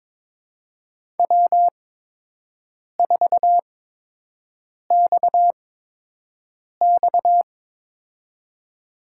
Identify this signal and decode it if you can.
Morse code, heard as a keyed tone, about 22 wpm, spaced as for 6 wpm, 700 Hz. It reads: W4XX
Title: USA Call Signs - 4 Characters - 8x Character Spacing - 22wpm